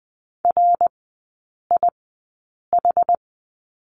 Morse code, heard as a keyed tone, about 20 wpm, spaced as for 8 wpm, 700 Hz. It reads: RIH
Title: 060 Lesson 11e - Letter h - Three Characters 20wpm